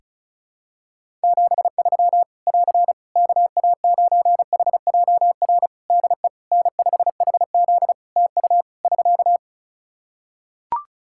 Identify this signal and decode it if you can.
Morse code, heard as a keyed tone, about 36 wpm, 700 Hz. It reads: 73 <AR> KA9HJR DE N5HZ TU <SK>
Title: Practice QSO - Exchange 4 - Ending - 35wpm